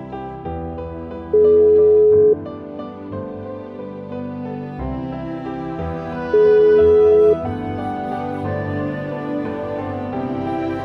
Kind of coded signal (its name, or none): telephone call progress tone